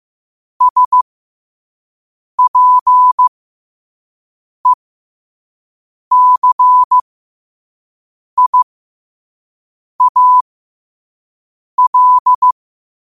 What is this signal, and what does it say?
Morse code, heard as a keyed tone, about 15 wpm, spaced as for 5 wpm, 1000 Hz. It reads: SPECIAL
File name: qotdfiles / 5wpm